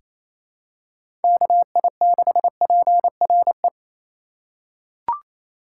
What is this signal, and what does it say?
Morse code, heard as a keyed tone, about 28 wpm, 700 Hz. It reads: KI6PRE
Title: USA Call Signs - 2x Character Spacing - 28wpm